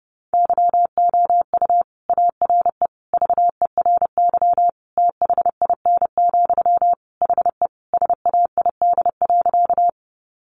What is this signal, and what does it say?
Morse code, heard as a keyed tone, about 30 wpm, 700 Hz. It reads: YOU ARE VERY THIN, HE SAID.
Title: Sentences from Top 1200 Words 30wpm